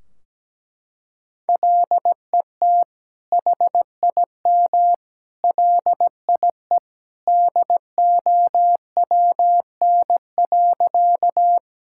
Morse code, heard as a keyed tone, about 17 wpm, 700 Hz. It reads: LET HIM LIE DOWN.
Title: Sentences from Top 1600 Words 17wpm